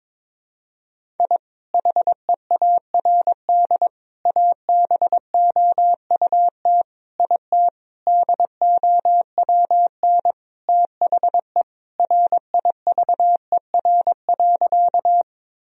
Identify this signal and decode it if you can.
Morse code, heard as a keyed tone, about 22 wpm, 700 Hz. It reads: I HEARD ABOUT IT DOWN THE RIVER.
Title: Sentences from Top 1000 Words - Review Part 1 - 22wpm